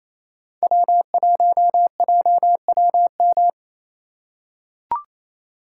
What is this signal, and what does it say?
Morse code, heard as a keyed tone, about 28 wpm, 700 Hz. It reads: W1JWM
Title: USA Call Signs - 2x Character Spacing - 28wpm